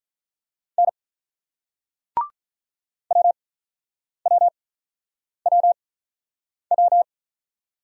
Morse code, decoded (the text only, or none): N WWWW